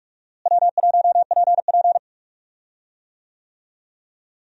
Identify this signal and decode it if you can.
Morse code, heard as a keyed tone, about 46 wpm, 700 Hz. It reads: W1PP
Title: USA Call Signs - 4 Characters - Repeated 2x - 45wpm